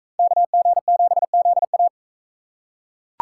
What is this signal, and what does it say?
Morse code, heard as a keyed tone, about 40 wpm, 700 Hz. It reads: KG7ZA